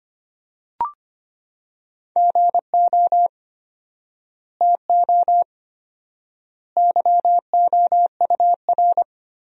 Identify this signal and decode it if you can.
Morse code, heard as a keyed tone, about 25 wpm, 700 Hz. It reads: GO TO YOUR
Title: 4x Word-Spacing - Sentences from Top 300 Words - 25wpm